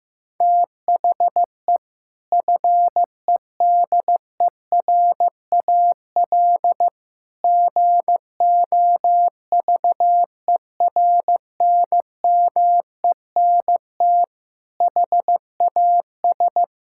Morse code, heard as a keyed tone, about 15 wpm, 700 Hz. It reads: THE FEDERAL GOVERNMENT HAS
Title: Sentences from Top 1000 Words - Review Part 5 - 15wpm